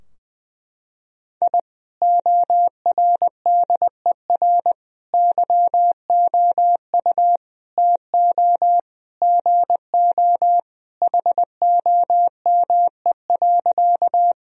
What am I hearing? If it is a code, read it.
Morse code, heard as a keyed tone, about 21 wpm, 700 Hz. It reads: I ORDER YOU TO GO HOME.
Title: Sentences from Top 1000 Words - Review Part 2 - 20wpm